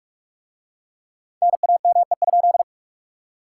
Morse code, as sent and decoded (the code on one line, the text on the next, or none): -. .- -- . ..--..
NAME?